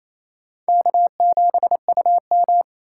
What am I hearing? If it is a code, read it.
Morse code, heard as a keyed tone, about 35 wpm, 700 Hz. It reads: K7UM
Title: Speed Racing - USA Call Signs 28wpm